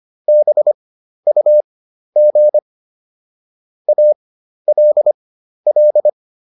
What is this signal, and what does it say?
Morse code, heard as a keyed tone, about 25 wpm, spaced as for 12 wpm, 600 Hz. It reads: BUG ALL